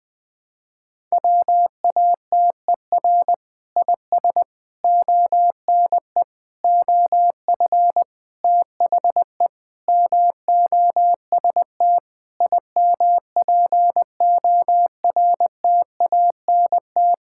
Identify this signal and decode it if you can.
Morse code, heard as a keyed tone, about 21 wpm, 700 Hz. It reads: WATER IS ONE OF THE MOST IMPORTANT
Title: Sentences with 5-Word Phrases - 20wpm